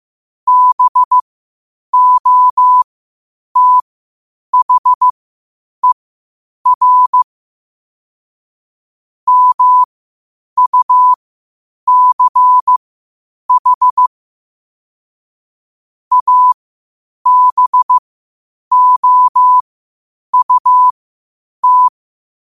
Morse code, heard as a keyed tone, about 15 wpm, spaced as for 8 wpm, 1000 Hz. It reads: BOTHER MUCH ABOUT
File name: qotdfiles / 8wpm